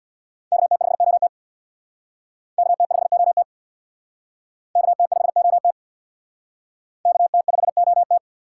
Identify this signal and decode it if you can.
Morse code, heard as a keyed tone, about 60 wpm, 700 Hz. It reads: KT5OT KT5OT KT5OT KT5OT
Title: Speed Racing - USA 2x2 Call Signs 50wpm